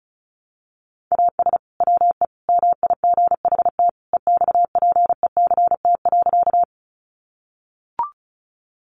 Morse code, heard as a keyed tone, about 34 wpm, 700 Hz. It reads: AS WE MIGHT EXPECT.
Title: Sentences from Top 1100 Words 35wpm